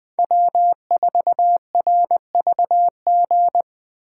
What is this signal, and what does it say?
Morse code, heard as a keyed tone, about 21 wpm, 700 Hz. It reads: W4RVG